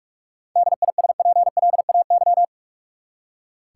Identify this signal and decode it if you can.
Morse code, heard as a keyed tone, about 44 wpm, 700 Hz. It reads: DISPLAY
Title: Seven Letter Words - 45wpm